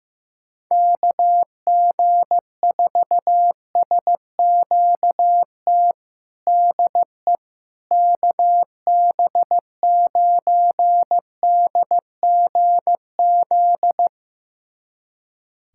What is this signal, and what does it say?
Morse code, heard as a keyed tone, about 15 wpm, 700 Hz. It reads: KG4SQT DE KB9DGZ
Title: USA Callsign DE Callsign - 6 Characters - 15wpm